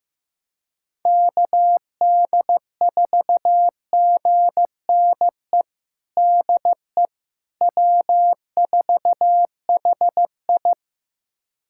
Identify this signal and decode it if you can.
Morse code, heard as a keyed tone, about 15 wpm, 700 Hz. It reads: KD4GNE DE W4HI
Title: USA Callsign DE Callsign 15wpm